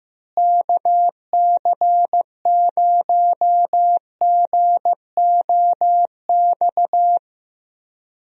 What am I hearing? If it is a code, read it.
Morse code, heard as a keyed tone, about 15 wpm, 700 Hz. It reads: KC0GOX
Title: USA Call Signs - 6 Characters - Repeated 3x - 15wpm